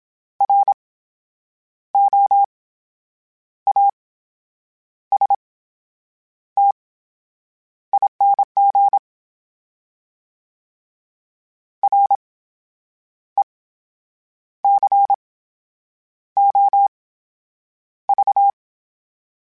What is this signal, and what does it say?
Morse code, heard as a keyed tone, about 27 wpm, spaced as for 8 wpm, 800 Hz. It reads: ROASTING RECOV
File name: Special INGs / Special INGs-mp3s / VerySlowToFaster Ing @25